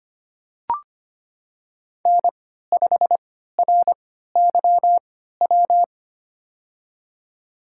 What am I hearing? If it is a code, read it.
Morse code, heard as a keyed tone, about 25 wpm, spaced as for 14 wpm, 700 Hz. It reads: N5RYW
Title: USA Call Signs - 3x Character Spacing - 25wpm